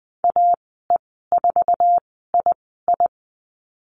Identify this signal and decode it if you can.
Morse code, heard as a keyed tone, about 20 wpm, spaced as for 15 wpm, 700 Hz. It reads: AE4II
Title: 048 Lesson 9g - Number 4 - Call Signs 20wpm